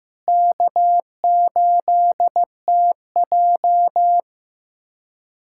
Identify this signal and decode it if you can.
Morse code, heard as a keyed tone, about 16 wpm, 700 Hz. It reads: K8TJ
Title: USA Callsign DE Callsign - 4 Characters - 15wpm